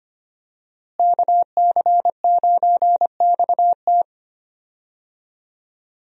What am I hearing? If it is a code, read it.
Morse code, heard as a keyed tone, about 25 wpm, 700 Hz. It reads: KC9XT
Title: USA Call Signs - 5 Characters - Repeated 2x - 25wpm